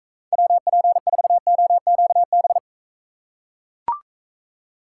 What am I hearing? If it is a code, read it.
Morse code, heard as a keyed tone, about 43 wpm, 700 Hz. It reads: WP4OQB